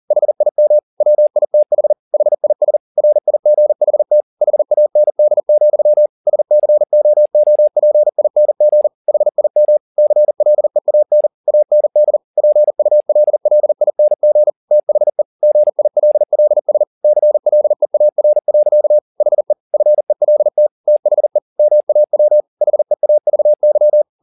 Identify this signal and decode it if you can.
Morse code, heard as a keyed tone, about 40 wpm, 600 Hz. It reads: HIM WITH HIS RIGHT HAND, SCOOPING HIM CLEAN AND PULLING THE GILLS CLEAR. HE FELT THE MAW HEAVY
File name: the old man and the sea 2 40wpm